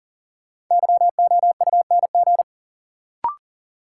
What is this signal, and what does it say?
Morse code, heard as a keyed tone, about 40 wpm, 700 Hz. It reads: YOUNG
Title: Sentences from Top 400 Words - New Word Drill - 40wpm